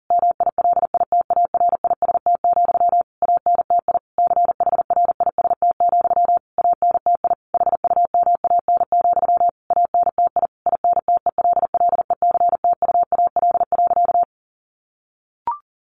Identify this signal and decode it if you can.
Morse code, heard as a keyed tone, about 40 wpm, 700 Hz. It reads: MILITARIST, ANTI CHRIST, ANTI HUMAN, ANTI INTELLECTUAL.
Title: A Sound Of Thunder 40wpm